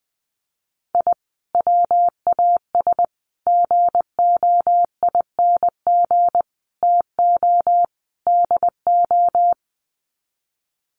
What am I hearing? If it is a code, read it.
Morse code, heard as a keyed tone, about 20 wpm, 700 Hz. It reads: I WAS GOING TO DO
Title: Sets of 5 Words Encore2 20wpm